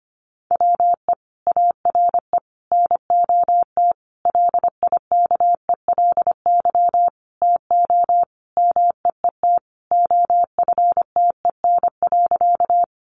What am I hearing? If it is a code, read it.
Morse code, heard as a keyed tone, about 25 wpm, 700 Hz. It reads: WE ARE NOT LIKELY TO MEET OFTEN.
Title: Sentences from Top 1100 Words 25wpm